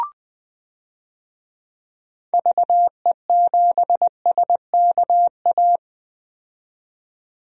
Morse code, heard as a keyed tone, about 20 wpm, 700 Hz. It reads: VE7SKA
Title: DXCC-100 Call Signs - 6 Characters - 20wpm